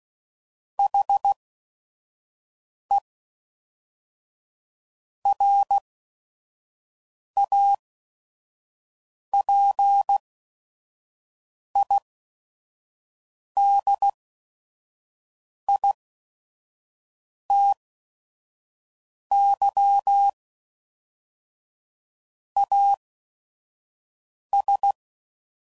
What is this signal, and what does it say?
Morse code, heard as a keyed tone, about 16 wpm, spaced as for 5 wpm, 800 Hz. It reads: HE RAPIDITY AS
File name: CW Code Groups @5WPM / Technical Subjects / XC